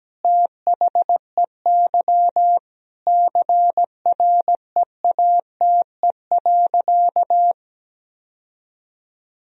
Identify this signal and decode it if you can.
Morse code, heard as a keyed tone, about 17 wpm, 700 Hz. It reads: THEY CREATE.
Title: Sentences from Top 1000 Words - Review Part 3 - 17wpm